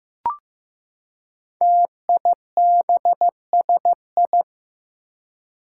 Morse code, heard as a keyed tone, about 15 wpm, 700 Hz. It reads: TIBSI
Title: Five Challenging Letter-Number Combinations 15wpm